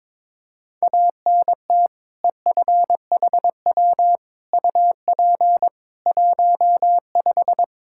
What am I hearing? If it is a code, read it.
Morse code, heard as a keyed tone, about 22 wpm, 700 Hz. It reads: ANT EFHW UP 15
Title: Practice QSO - Exchange 2 - Info - 22wpm